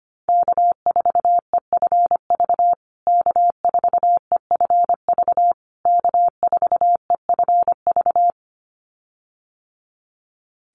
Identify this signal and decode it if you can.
Morse code, heard as a keyed tone, about 25 wpm, 700 Hz. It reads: K4EFV K4EFV K4EFV
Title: USA Call Signs Repeated 3x 25wpm